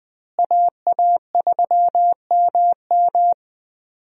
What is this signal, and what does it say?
Morse code, heard as a keyed tone, about 20 wpm, 700 Hz. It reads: AA3MM
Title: USA Call Signs - 5 Characters - Repeated 3x - 20wpm